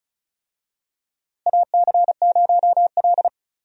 Morse code, heard as a keyed tone, about 36 wpm, 700 Hz. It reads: AC0L
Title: USA Call Signs - 4 Characters - Repeated 3x - 35wpm